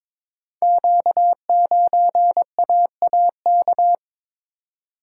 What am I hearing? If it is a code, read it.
Morse code, heard as a keyed tone, about 22 wpm, 700 Hz. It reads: Q9AAK
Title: Code Groups - 5 Letter:Numbers - 22wpm